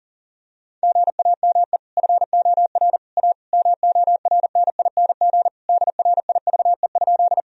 Morse code, heard as a keyed tone, about 40 wpm, 700 Hz. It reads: GAME FOR A MORNING DRIVE?
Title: Sentences from Top 1000 Words - Review Part 3 - 40wpm